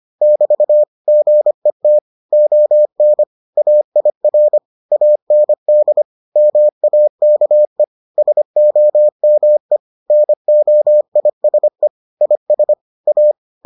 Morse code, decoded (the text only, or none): <BT> GET ON AIR AND MAKE SOME NOISE IS A